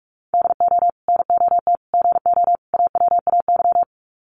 Morse code, heard as a keyed tone, about 44 wpm, 700 Hz. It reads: DO NOT GO AWAY
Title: Sets of 4 Words 45wpm